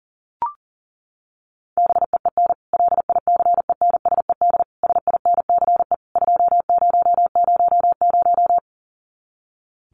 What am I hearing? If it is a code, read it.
Morse code, heard as a keyed tone, about 40 wpm, 700 Hz. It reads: BEEN LICENSED SINCE 2000
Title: Common QSO Phrases 40wpm